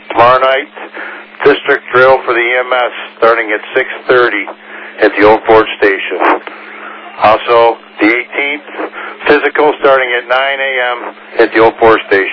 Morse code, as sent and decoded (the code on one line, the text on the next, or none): none